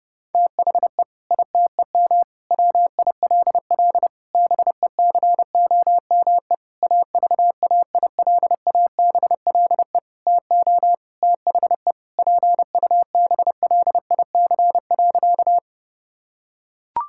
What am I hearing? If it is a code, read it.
Morse code, heard as a keyed tone, about 32 wpm, 700 Hz. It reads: THE ITEM WILL BECOME AVAILABLE TO THE PUBLIC.
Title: Sentences from Top 1000 Words - Review Part 6 - 30wpm